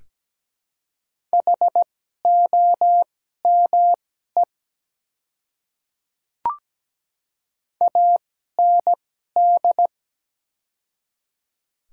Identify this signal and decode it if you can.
Morse code, heard as a keyed tone, about 17 wpm, spaced as for 12 wpm, 700 Hz. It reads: HOME AND
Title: 2x Character Spacing - Top 200 Words - 17wpm